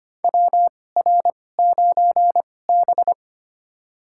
Morse code, heard as a keyed tone, about 25 wpm, spaced as for 18 wpm, 700 Hz. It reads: WR9B
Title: USA Call Signs - 2x Character Spacing - 25wpm